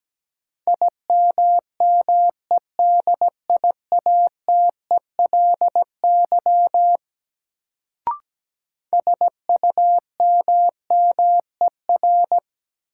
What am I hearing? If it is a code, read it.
Morse code, heard as a keyed tone, about 17 wpm, 700 Hz. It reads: IMMEDIATELY SUMMER
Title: Sentences from Top 1100 Words - New Word Drill Alt - 17wpm